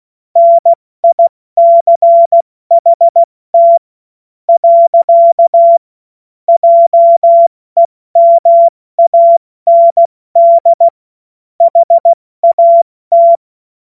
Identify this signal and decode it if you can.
Morse code, heard as a keyed tone, about 16 wpm, spaced as for 13 wpm, 680 Hz. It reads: NICHT. JEMAND HAT
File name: Morsekurs/Witze